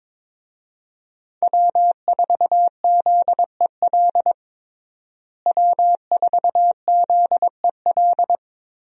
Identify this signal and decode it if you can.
Morse code, heard as a keyed tone, about 22 wpm, 700 Hz. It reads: W4ZEL W4ZEL
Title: USA Call Signs - 5 Characters - Repeated 2x - 22wpm